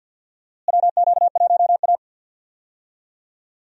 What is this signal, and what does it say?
Morse code, heard as a keyed tone, about 52 wpm, 700 Hz. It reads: WQ1A
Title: USA Call Signs - 4 Characters - Repeated 2x - 50wpm